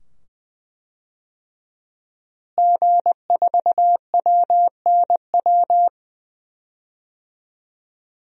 Morse code, heard as a keyed tone, about 20 wpm, 700 Hz. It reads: G4WNW
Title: DXCC-100 Call Signs - 5 Characters - Repeated 2x - 20wpm